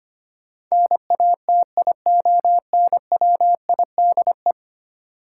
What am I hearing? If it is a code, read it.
Morse code, heard as a keyed tone, about 25 wpm, 700 Hz. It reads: NATIONWIDE
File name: 10 Letter Words - First 3 Letters Given - 25wpm